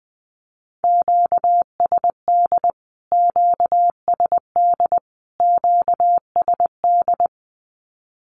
Morse code, heard as a keyed tone, about 20 wpm, 700 Hz. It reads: QSD QSD QSD
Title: Q-Codes Repeated 3x 20wpm